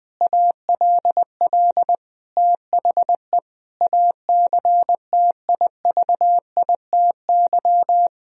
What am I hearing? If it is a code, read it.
Morse code, heard as a keyed tone, about 21 wpm, 700 Hz. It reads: ALL THE ACTIVITY
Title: Sentences from Top 1000 Words - Review Part 4 - 20wpm